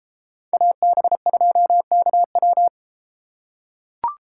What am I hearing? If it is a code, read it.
Morse code, heard as a keyed tone, about 31 wpm, 700 Hz. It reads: AB2KW